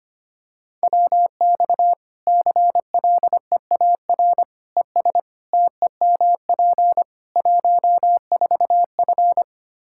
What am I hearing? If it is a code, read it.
Morse code, heard as a keyed tone, about 25 wpm, 700 Hz. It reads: WX CLEAR ES TEMP 14F
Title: K1USN SST QSO Practice 25wpm